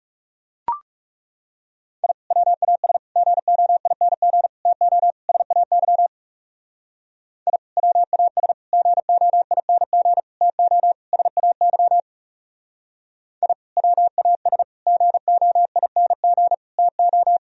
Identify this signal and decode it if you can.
Morse code, heard as a keyed tone, about 36 wpm, 700 Hz. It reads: I WAS GOING TO SAY I WAS GOING TO SAY I WAS GOING TO
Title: Speed-Racing - Sets of 5 Words 30wpm